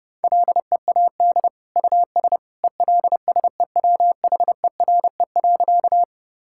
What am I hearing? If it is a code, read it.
Morse code, heard as a keyed tone, about 30 wpm, 700 Hz. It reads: LEAD US ELSEWHERE.
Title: Sentences from Top 1600 Words 30wpm